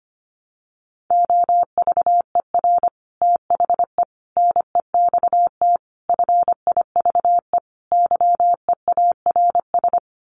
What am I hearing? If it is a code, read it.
Morse code, heard as a keyed tone, about 25 wpm, 700 Hz. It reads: OVER THE NEXT FIVE YEARS
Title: Sets of 5 Words 25wpm